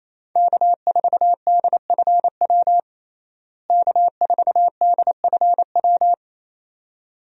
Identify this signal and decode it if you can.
Morse code, heard as a keyed tone, about 28 wpm, 700 Hz. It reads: K4DFW K4DFW
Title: USA Call Signs - 5 Characters - Repeated 2x - 28wpm